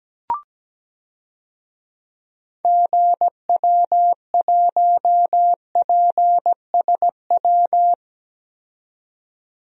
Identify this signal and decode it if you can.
Morse code, heard as a keyed tone, about 17 wpm, 700 Hz. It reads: GW1PSW
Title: DXCC-100 Call Signs - 6 Characters - 17wpm